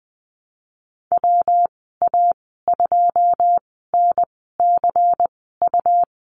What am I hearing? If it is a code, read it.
Morse code, heard as a keyed tone, about 20 wpm, spaced as for 15 wpm, 700 Hz. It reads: WA2NCU
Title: 167 Lesson 27g - Number 7 - Call Signs 20wpm